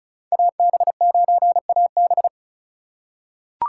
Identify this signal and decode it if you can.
Morse code, heard as a keyed tone, about 36 wpm, 700 Hz. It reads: AB9AB
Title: USA Call Signs - 5 Characters - 3x Character Spacing - 35wpm